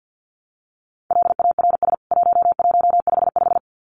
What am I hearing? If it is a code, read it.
Morse code, heard as a keyed tone, about 50 wpm, 700 Hz. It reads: LARS 1155
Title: CWT Exchange 50wpm